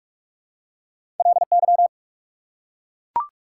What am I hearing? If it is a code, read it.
Morse code, heard as a keyed tone, about 44 wpm, 690 Hz. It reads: LY
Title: Prefixes and Suffixes 45wpm